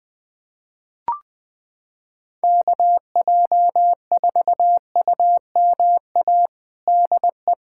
Morse code, heard as a keyed tone, about 20 wpm, 700 Hz. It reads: KJ4UMA DE
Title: USA Callsign DE Callsign - 6 Characters - 20wpm